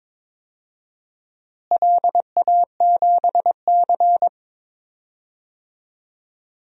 Morse code, heard as a keyed tone, about 22 wpm, 700 Hz. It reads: LA7C
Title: DXCC-100 Call Signs - 4 Characters - Repeated 2x - 22wpm